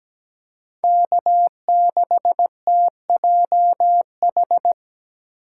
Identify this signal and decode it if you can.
Morse code, heard as a keyed tone, about 17 wpm, 700 Hz. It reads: K6TJH
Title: USA Call Signs - 5 Characters - 8x Character Spacing - 17wpm